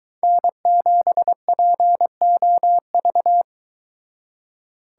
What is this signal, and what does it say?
Morse code, heard as a keyed tone, about 23 wpm, 700 Hz. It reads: N7POV